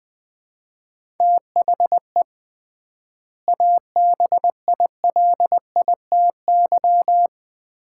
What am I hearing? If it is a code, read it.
Morse code, heard as a keyed tone, about 21 wpm, 700 Hz. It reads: THE ABILITY
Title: 3x Word-Spacing - Sets of 3 Words - 20wpm